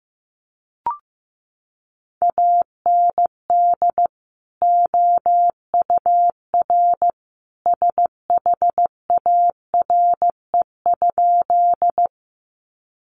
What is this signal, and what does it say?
Morse code, heard as a keyed tone, about 15 wpm, 700 Hz. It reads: AND OUR SHARE?
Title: Sentences from Top 900 Words 15wpm